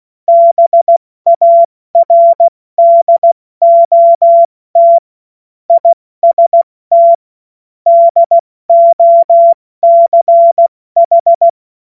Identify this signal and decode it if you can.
Morse code, heard as a keyed tone, about 16 wpm, 680 Hz. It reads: BARDOT IST DOCH
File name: Morsekurs / Witze / Witz 04. Januar 2022, 70 F80